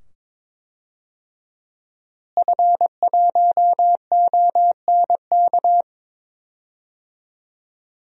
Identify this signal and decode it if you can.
Morse code, heard as a keyed tone, about 22 wpm, 700 Hz. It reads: F1ONK
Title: DXCC-100 Call Signs - 5 Characters - Repeated 2x - 22wpm